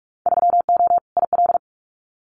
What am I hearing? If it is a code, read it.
Morse code, heard as a keyed tone, about 46 wpm, 700 Hz. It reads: 3O IL
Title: Winter Field Day Exchanges 45wpm